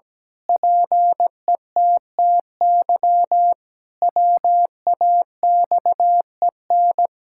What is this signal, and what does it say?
Morse code, heard as a keyed tone, about 17 wpm, 700 Hz. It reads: PETTY WAXEN
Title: Pangram Challenge v2 - 17wpm